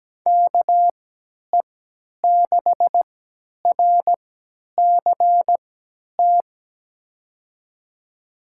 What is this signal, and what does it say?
Morse code, heard as a keyed tone, about 17 wpm, spaced as for 10 wpm, 700 Hz. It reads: KE6RCT
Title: USA Call Signs - 6 Characters - 3x Character Spacing - 17wpm